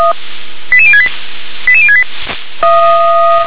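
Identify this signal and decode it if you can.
Five-tone selective call (ZVEI), heard as quick runs of five tones, tones of about 70 ms. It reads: 88068 88068
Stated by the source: ZVEI five-tone selective call 88020 received by SDR.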